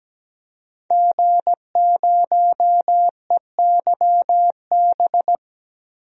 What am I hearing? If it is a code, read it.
Morse code, heard as a keyed tone, about 17 wpm, 700 Hz. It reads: G0EYB
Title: DXCC-100 Call Signs - 5 Characters - 17wpm